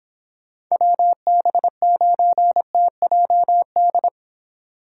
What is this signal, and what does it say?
Morse code, heard as a keyed tone, about 29 wpm, 700 Hz. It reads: WB9TJD